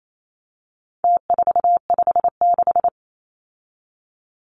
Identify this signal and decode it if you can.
Morse code, heard as a keyed tone, about 28 wpm, 700 Hz. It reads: T456
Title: Numbers - Contest Style - 4-Digit 28wpm